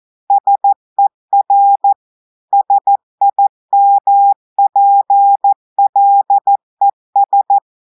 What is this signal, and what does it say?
Morse code, heard as a keyed tone, about 14 wpm, 800 Hz. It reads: SER SIMPLES
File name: audios / portugais / portugais texte en clair 2 vitesse 14 mots minutes